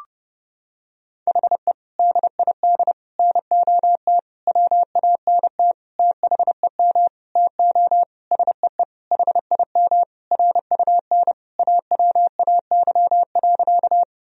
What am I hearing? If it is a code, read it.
Morse code, heard as a keyed tone, about 29 wpm, 700 Hz. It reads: HE DID NOT WANT THEM TO SEE HIM RUN AWAY.
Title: Sentences from Top 1000 Words - Review Part 1 - 30wpm